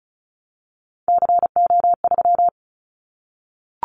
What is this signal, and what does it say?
Morse code, heard as a keyed tone, about 34 wpm, 700 Hz. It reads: CO3
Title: Callsign Prefixes for Top 100 DXCC - 35wpm